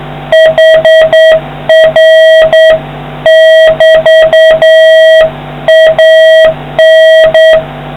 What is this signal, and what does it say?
Morse code, heard as a keyed tone, about 9 wpm, 1900 Hz, with noise. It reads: HR<BT>AN